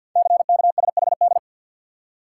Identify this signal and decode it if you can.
Morse code, heard as a keyed tone, about 53 wpm, 700 Hz. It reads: CKSHD
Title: Code Groups - 5 Letters - 50wpm